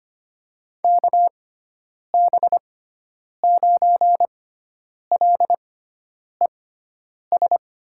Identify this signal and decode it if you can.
Morse code, heard as a keyed tone, about 25 wpm, spaced as for 9 wpm, 700 Hz. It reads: KB9LES
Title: USA Call Signs - 6 Characters - 6x Character Spacing - 25wpm